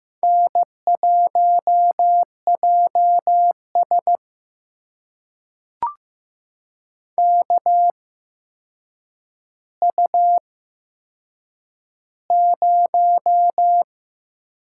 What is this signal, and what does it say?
Morse code, heard as a keyed tone, about 15 wpm, spaced as for 7 wpm, 700 Hz. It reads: N1JS KU0
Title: USA Call Signs - 4 Characters - 8x Character Spacing - 15wpm